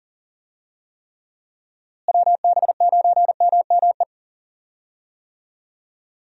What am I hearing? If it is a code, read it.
Morse code, heard as a keyed tone, about 40 wpm, 700 Hz. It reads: WB9MME
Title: USA Call Signs - 6 Characters - Repeated 3x - 40wpm